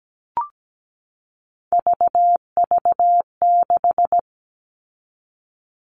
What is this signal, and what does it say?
Morse code, heard as a keyed tone, about 17 wpm, 700 Hz. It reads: VV6
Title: All Cut Numbers - 3-Digit 17wpm